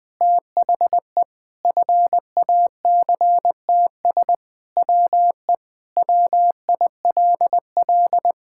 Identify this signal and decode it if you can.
Morse code, heard as a keyed tone, about 20 wpm, 700 Hz. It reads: THE FACTS WE WILL
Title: Sentences with 3-Word Phrases - Spoken Beginning - 20wpm